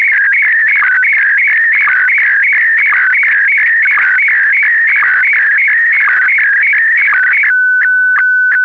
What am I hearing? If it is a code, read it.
SSTV (slow-scan television), Scottie DX.
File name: sstv / sstv2 / scottie dx